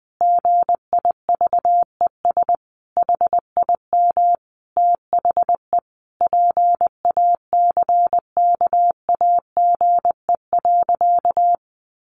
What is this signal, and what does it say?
Morse code, heard as a keyed tone, about 19 wpm, 700 Hz. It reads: GIVES HIM THE PACKAGE.
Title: Additional Easy Phrases 20wpm